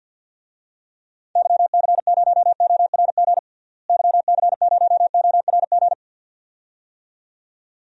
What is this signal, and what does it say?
Morse code, heard as a keyed tone, about 52 wpm, 700 Hz. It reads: YC0ORG YC0ORG
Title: DXCC-100 Call Signs - 6 Characters - Repeated 2x - 50wpm